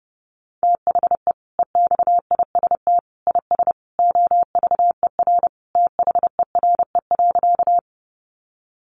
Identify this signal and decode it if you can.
Morse code, heard as a keyed tone, about 30 wpm, 700 Hz. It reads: THE EXIST IS OVER THERE.
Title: Sentences from Top 1100 Words 30wpm